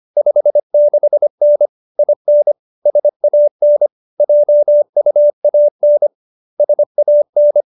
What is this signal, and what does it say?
Morse code, heard as a keyed tone, about 25 wpm, 590 Hz. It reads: 56N IN SAN JUAN SAN